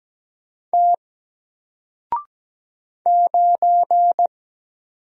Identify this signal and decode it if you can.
Morse code, heard as a keyed tone, about 17 wpm, 700 Hz. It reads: T 9
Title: Single Character (Letter:Number:Punctuation:Prosign) Alt 17wpm